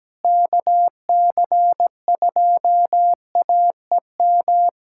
Morse code, heard as a keyed tone, about 17 wpm, 700 Hz. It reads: KC2AEM
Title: USA Callsign DE Callsign 17wpm